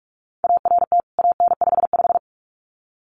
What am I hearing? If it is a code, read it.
Morse code, heard as a keyed tone, about 44 wpm, 700 Hz. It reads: ART AN55
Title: CWT Exchange 45wpm